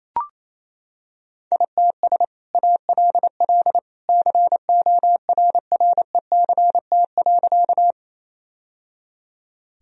Morse code, heard as a keyed tone, about 27 wpm, 700 Hz. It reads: ITS ALL CORRECT.
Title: Sentences from Top 1000 Words - Review Part 2 - 28wpm